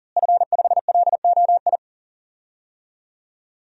Morse code, heard as a keyed tone, about 40 wpm, 700 Hz. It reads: F5LOI
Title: DXCC-100 Call Signs - 5 Characters - Repeated 3x - 40wpm